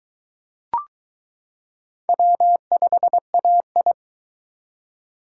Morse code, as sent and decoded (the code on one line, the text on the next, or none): .-- ..... .- ..
W5AI